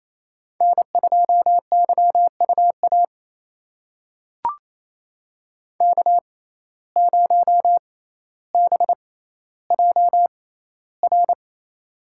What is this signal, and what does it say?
Morse code, heard as a keyed tone, about 28 wpm, spaced as for 14 wpm, 700 Hz. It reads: N2YUA K0BJR
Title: USA Call Signs - 5 Characters - 6x Character Spacing - 28wpm